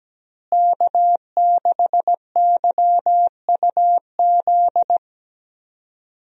Morse code, heard as a keyed tone, about 17 wpm, 700 Hz. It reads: K6YUZ
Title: USA Call Signs - 5 Characters - Repeated 2x - 17wpm